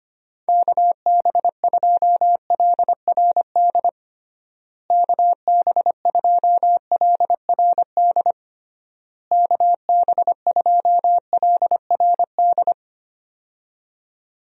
Morse code, heard as a keyed tone, about 25 wpm, 700 Hz. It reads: KB2LRD KB2LRD KB2LRD
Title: USA Call Signs - 6 Characters - Repeated 3x - 25wpm